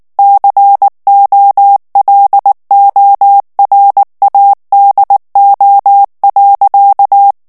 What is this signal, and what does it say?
Morse code, heard as a keyed tone, about 19 wpm, 800 Hz. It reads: COLORADO.
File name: ANGLAIS texte clair 3 vitesse 18 mots minute